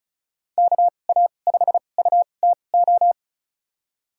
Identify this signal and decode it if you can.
Morse code, heard as a keyed tone, about 36 wpm, spaced as for 25 wpm, 700 Hz. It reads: KA5UTO
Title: USA Call Signs - 6 Characters - 2x Character Spacing - 35wpm